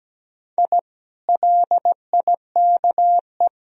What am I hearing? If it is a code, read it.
Morse code, heard as a keyed tone, about 17 wpm, 700 Hz. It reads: I LIKE
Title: Sentences from Top 1000 Words - Review Part 4 - 17wpm